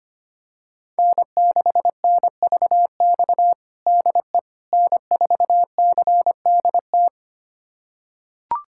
Morse code, heard as a keyed tone, about 25 wpm, 700 Hz. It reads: N6NVX DE N4CDT
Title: USA Callsign DE Callsign - 5 Characters - 25wpm